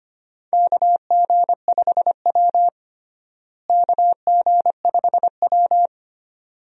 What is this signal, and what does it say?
Morse code, heard as a keyed tone, about 25 wpm, 700 Hz. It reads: KG5W KG5W
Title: USA Call Signs - 4 Characters - Repeated 2x - 25wpm